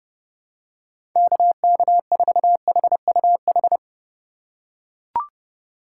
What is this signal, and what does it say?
Morse code, heard as a keyed tone, about 30 wpm, 700 Hz. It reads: KK4HUH
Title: USA Call Signs - 6 Characters - 4x Character Spacing - 30wpm